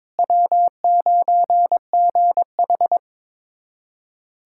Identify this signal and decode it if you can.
Morse code, heard as a keyed tone, about 22 wpm, 700 Hz. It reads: W9GH
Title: USA Callsign DE Callsign - 4 Characters - 22wpm